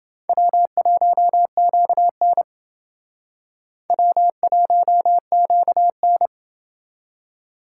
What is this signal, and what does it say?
Morse code, heard as a keyed tone, about 29 wpm, 700 Hz. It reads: W1QN W1QN